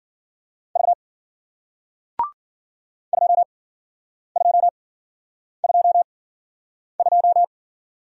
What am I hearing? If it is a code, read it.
Morse code, heard as a keyed tone, about 56 wpm, spaced as for 9 wpm, 700 Hz. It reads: V 2222